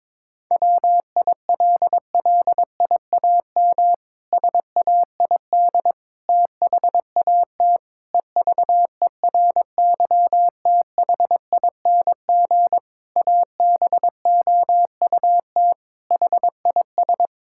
Morse code, heard as a keyed tone, about 22 wpm, 700 Hz. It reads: WILLIAM SAID THAT EVERYTHING ABOUT HIS
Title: Pangram Challenge v2 - 22wpm